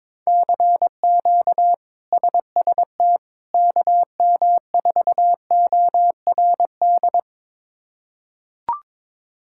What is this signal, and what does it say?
Morse code, heard as a keyed tone, about 22 wpm, 700 Hz. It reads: CQ SST KM4ORD
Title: K1USN SST QSO Practice - 22wpm